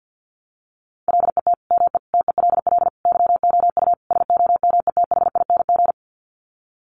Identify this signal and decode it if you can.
Morse code, heard as a keyed tone, about 50 wpm, 700 Hz. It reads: LET ME TELL YOU SOMETHING
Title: Sets of 5 Words 50wpm